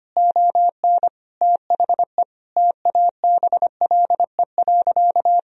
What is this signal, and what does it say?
Morse code, heard as a keyed tone, about 25 wpm, 700 Hz. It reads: ON THE TABLE.
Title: Sentences from Top 1000 Words - Review Part 4 - 25wpm